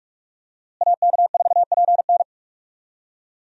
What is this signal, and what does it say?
Morse code, heard as a keyed tone, about 46 wpm, 700 Hz. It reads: AK4PN
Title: USA Call Signs - 5 Characters - Repeated 2x - 45wpm